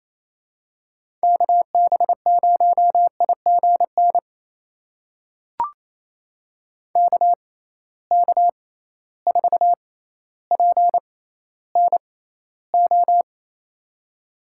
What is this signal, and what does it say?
Morse code, heard as a keyed tone, about 27 wpm, spaced as for 14 wpm, 700 Hz. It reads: KB0IGN KK4PNO
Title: USA Call Signs - 6 Characters - 6x Character Spacing - 28wpm